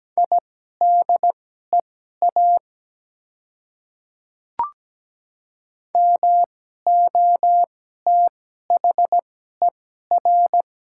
Morse code, heard as a keyed tone, about 17 wpm, spaced as for 12 wpm, 700 Hz. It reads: IDEA MOTHER
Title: 2x Character Spacing - Top 300 Words - 17wpm